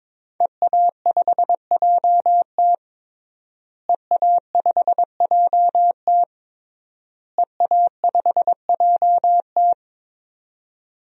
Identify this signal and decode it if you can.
Morse code, heard as a keyed tone, about 22 wpm, 700 Hz. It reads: EA5JT EA5JT EA5JT
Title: DXCC-100 Call Signs - 5 Characters - Repeated 3x - 22wpm